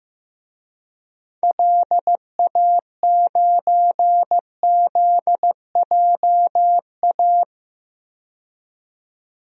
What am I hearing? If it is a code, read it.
Morse code, heard as a keyed tone, about 15 wpm, 700 Hz. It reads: LA9ZJA
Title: DXCC-100 Call Signs - 6 Characters - 15wpm